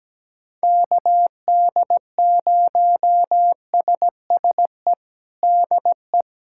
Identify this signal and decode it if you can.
Morse code, heard as a keyed tone, about 17 wpm, 700 Hz. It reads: KD0SSE DE
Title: USA Callsign DE Callsign 17wpm